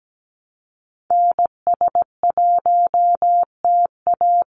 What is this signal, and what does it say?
Morse code, heard as a keyed tone, about 17 wpm, 700 Hz. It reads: NS1TA
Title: NA SOTA Activator Callsigns 17wpm